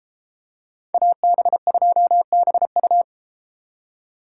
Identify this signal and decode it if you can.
Morse code, heard as a keyed tone, about 33 wpm, 700 Hz. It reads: AB2BU